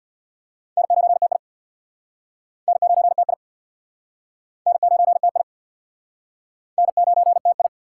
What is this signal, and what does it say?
Morse code, heard as a keyed tone, about 65 wpm, 700 Hz. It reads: N9TI N9TI N9TI N9TI